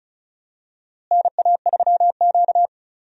Morse code, heard as a keyed tone, about 36 wpm, 700 Hz. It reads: NA3Q
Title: USA Call Signs - 4 Characters - Repeated 3x - 35wpm